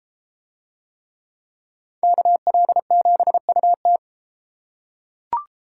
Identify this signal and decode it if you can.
Morse code, heard as a keyed tone, about 32 wpm, 700 Hz. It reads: KL7UT